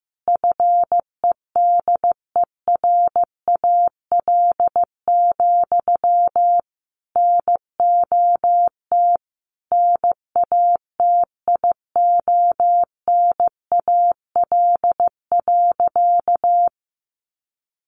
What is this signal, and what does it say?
Morse code, heard as a keyed tone, about 15 wpm, 700 Hz. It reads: FEDERAL, NOT NATIONAL.
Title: Sentences from Top 1200 Words 15wpm